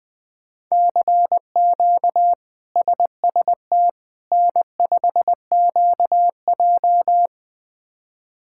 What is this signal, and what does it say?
Morse code, heard as a keyed tone, about 21 wpm, 700 Hz. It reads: CQ SST N5QJ
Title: K1USN SST QSO Practice - 20wpm